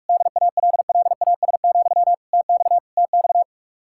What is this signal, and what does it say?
Morse code, heard as a keyed tone, about 48 wpm, 700 Hz. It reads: DALLAS, TX TX